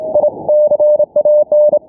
Morse code, heard as a keyed tone, about 28 wpm, 600 Hz. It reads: I CAN
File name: n6tr / HomeBrewTransmitter / N6TR 0120z 7019kHz